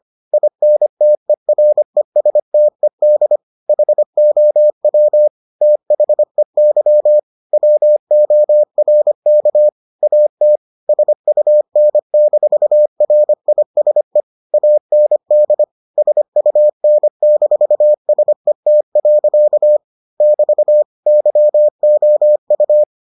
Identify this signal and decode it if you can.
Morse code, heard as a keyed tone, about 25 wpm, 600 Hz. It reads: INTERESTED HOW THEY WORK AT SUN-RISE AND SUN-SET. <BT> YOU